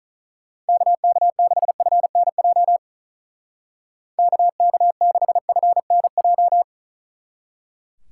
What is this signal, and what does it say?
Morse code, heard as a keyed tone, about 46 wpm, 700 Hz. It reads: KK6FNJ KK6FNJ